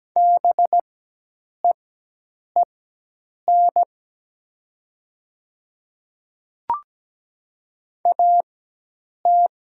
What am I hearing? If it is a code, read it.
Morse code, heard as a keyed tone, about 17 wpm, spaced as for 8 wpm, 700 Hz. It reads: BEEN AT
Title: 4x Character Spacing - Top 200 Words - 17wpm